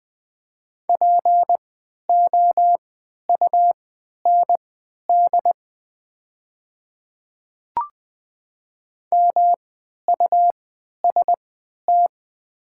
Morse code, heard as a keyed tone, about 20 wpm, spaced as for 11 wpm, 700 Hz. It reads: POUND MUST
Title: 3x Character Spacing - Top 500 Words - 20wpm